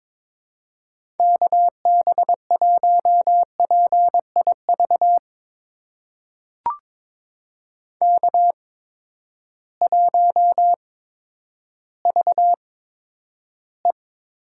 Morse code, heard as a keyed tone, about 22 wpm, spaced as for 11 wpm, 700 Hz. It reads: KB1PIV K1VE
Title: USA Call Signs - 8x Character Spacing - 22wpm